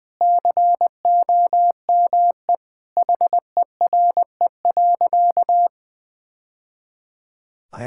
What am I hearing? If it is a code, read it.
Morse code, heard as a keyed tone, about 20 wpm, 700 Hz. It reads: COME HERE.
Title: Sentences with 2-Word Phrases - Spoken Beginning - 20wpm